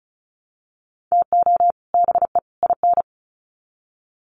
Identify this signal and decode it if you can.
Morse code, heard as a keyed tone, about 34 wpm, 700 Hz. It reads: TO BE IN
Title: Sets of 3 Words 35wpm